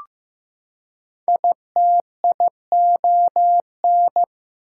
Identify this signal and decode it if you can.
Morse code, heard as a keyed tone, about 15 wpm, 700 Hz. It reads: ITION
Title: Prefixes and Suffixes 15wpm